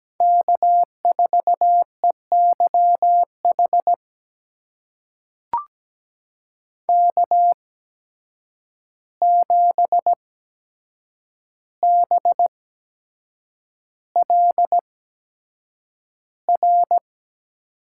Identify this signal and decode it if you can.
Morse code, heard as a keyed tone, about 17 wpm, spaced as for 7 wpm, 700 Hz. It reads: K4EYH K7BLR
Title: USA Call Signs - 5 Characters - 8x Character Spacing - 17wpm